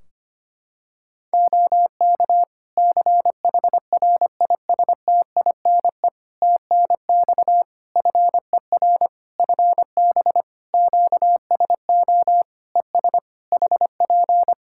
Morse code, heard as a keyed tone, about 25 wpm, 700 Hz. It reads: OK CHRISTINE TNX FER FB QSO ES HP
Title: Practice QSO - Exchange 4 - Ending - 25wpm